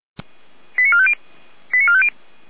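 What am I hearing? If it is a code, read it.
Five-tone selective call (ZVEI), heard as quick runs of five tones, tones of about 70 ms. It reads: 89450 89450
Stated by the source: ZVEI five-tone selective call, SDR reception.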